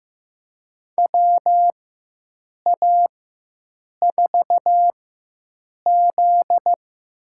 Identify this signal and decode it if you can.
Morse code, heard as a keyed tone, about 15 wpm, spaced as for 7 wpm, 700 Hz. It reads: WA4Z
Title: USA Call Signs - 4x Character Spacing - 15wpm